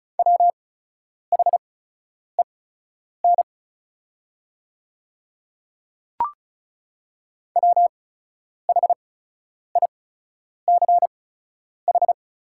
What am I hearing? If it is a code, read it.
Morse code, heard as a keyed tone, about 36 wpm, spaced as for 10 wpm, 690 Hz. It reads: WHEN WHICH
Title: 8x Character Spacing - Top 100 Words - 35wpm